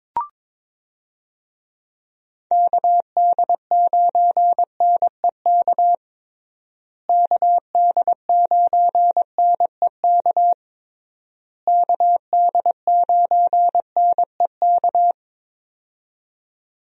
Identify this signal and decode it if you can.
Morse code, heard as a keyed tone, about 21 wpm, 700 Hz. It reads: KD9NEK KD9NEK KD9NEK
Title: USA Call Signs - 6 Characters - Repeated 3x - 22wpm